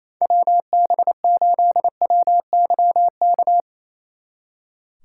Morse code, heard as a keyed tone, about 34 wpm, 700 Hz. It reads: WB8WYK
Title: Speed Racing - USA 2x3 Call Signs 28wpm